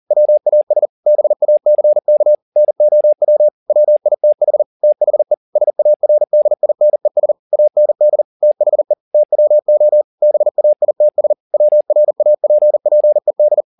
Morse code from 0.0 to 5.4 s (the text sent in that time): WAS BACK NOW WITH THE